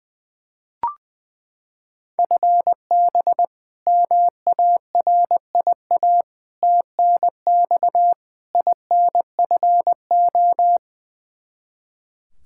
Morse code, heard as a keyed tone, about 20 wpm, 700 Hz. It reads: FB MARIA TNX INFO
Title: Practice QSO - Exchange 3 - More Info - 20wpm